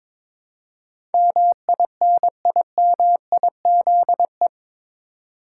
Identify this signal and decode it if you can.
Morse code, heard as a keyed tone, about 22 wpm, 700 Hz. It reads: MINIMIZE
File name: practice / 22wpm / Eight Letter Words - 22wpm